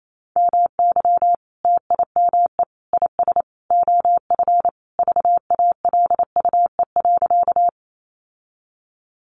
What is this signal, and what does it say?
Morse code, heard as a keyed tone, about 28 wpm, 700 Hz. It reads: MY TIME IS OF VALUE.
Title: Sentences from Top 600 Words 28wpm